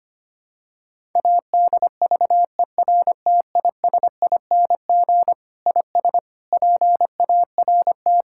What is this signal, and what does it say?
Morse code, heard as a keyed tone, about 25 wpm, 700 Hz. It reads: ADVERTISING IS PART
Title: Sentences from Top 1000 Words - Review Part 6 - 25wpm